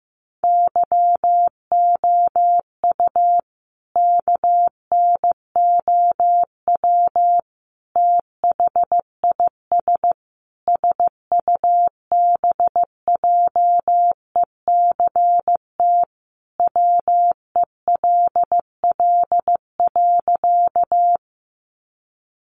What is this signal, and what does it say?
Morse code, heard as a keyed tone, about 15 wpm, 700 Hz. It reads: YOU KNOW THIS SUBJECT WELL.
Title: Sentences from Top 800 Words 15wpm